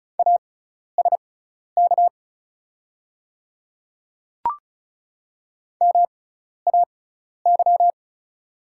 Morse code, heard as a keyed tone, about 34 wpm, spaced as for 12 wpm, 700 Hz. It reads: ASK MAY
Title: 6x Character Spacing - Top 300 Words - 35wpm